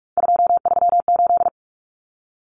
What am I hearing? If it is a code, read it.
Morse code, heard as a keyed tone, about 44 wpm, 700 Hz. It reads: 238
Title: Common Cut Numbers - 3-Digit 45wpm